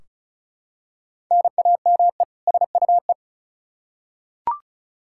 Morse code, heard as a keyed tone, about 36 wpm, 700 Hz. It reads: NAME SUE
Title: Practice QSO - Exchange 1 - Beginning - 35wpm